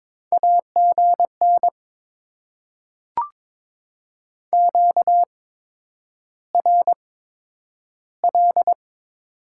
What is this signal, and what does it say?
Morse code, heard as a keyed tone, about 22 wpm, spaced as for 9 wpm, 700 Hz. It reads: AGN QRL
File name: Top 20 QSO Elements - 8x Character Spacing - 22wpm